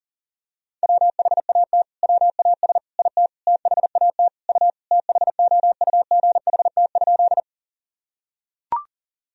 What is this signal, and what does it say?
Morse code, heard as a keyed tone, about 40 wpm, 700 Hz. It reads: WHAT WAS IT THAT U THOUGHT?
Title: Sentences from Top 300 Words and Abbreviations 40wpm